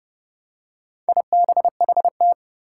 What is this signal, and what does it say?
Morse code, heard as a keyed tone, about 30 wpm, 700 Hz. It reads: IBHT
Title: Four Challenging Letter-Number Combinations 30wpm